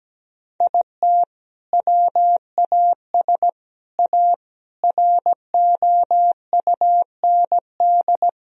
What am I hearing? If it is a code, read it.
Morse code, heard as a keyed tone, about 17 wpm, 700 Hz. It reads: IT WAS A ROUND
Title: Sentences from Top 1000 Words - Review Part 2 - 17wpm